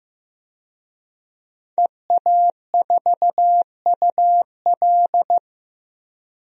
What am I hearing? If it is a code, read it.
Morse code, heard as a keyed tone, about 15 wpm, 700 Hz. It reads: EA4UL
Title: DXCC-100 Call Signs - 5 Characters - 15wpm